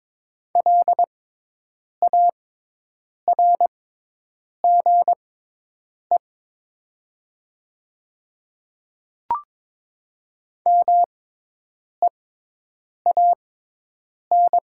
Morse code, heard as a keyed tone, about 22 wpm, spaced as for 8 wpm, 700 Hz. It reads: LARGE MEAN
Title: 6x Character Spacing - Top 200 Words - 22wpm